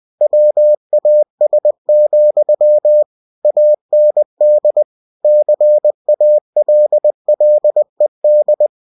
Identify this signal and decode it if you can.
Morse code, heard as a keyed tone, about 20 wpm, 600 Hz. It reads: WAS, AND CALLED
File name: cw-books / winnie-the-pooh / winnie-20wpm-page-0053